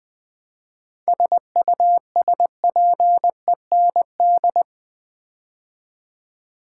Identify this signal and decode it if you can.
Morse code, heard as a keyed tone, about 20 wpm, 700 Hz. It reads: SUSPEND
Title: 7 Letter Words - First 3 Letters Given - 20wpm